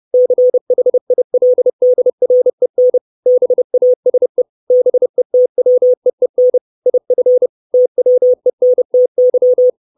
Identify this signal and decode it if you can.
Morse code, heard as a keyed tone, about 30 wpm, 500 Hz. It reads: CHILDREN BASE BETWEEN IF TWENTY
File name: 632 Words 30WPM